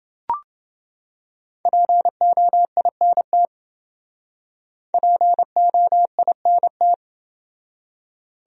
Morse code, heard as a keyed tone, about 28 wpm, 700 Hz. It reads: POINT POINT